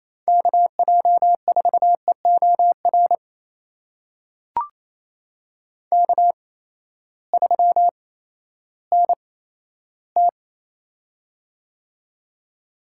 Morse code, heard as a keyed tone, about 28 wpm, spaced as for 14 wpm, 700 Hz. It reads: KJ4EOR K3NT
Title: USA Call Signs - 8x Character Spacing - 28wpm